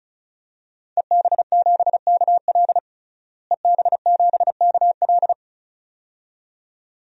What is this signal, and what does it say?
Morse code, heard as a keyed tone, about 34 wpm, 700 Hz. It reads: EB7KL EB7KL
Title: DXCC-100 Call Signs - 5 Characters - Repeated 2x - 35wpm